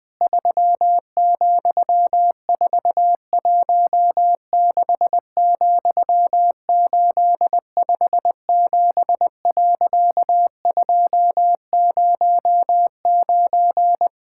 Morse code, heard as a keyed tone, about 20 wpm, 700 Hz. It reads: 3,416,857.209
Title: Pangram Challenge v2 - 20wpm